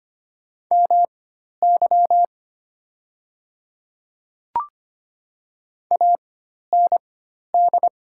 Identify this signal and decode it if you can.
Morse code, heard as a keyed tone, about 25 wpm, spaced as for 12 wpm, 700 Hz. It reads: MY AND
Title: 4x Character Spacing - Top 200 Words - 25wpm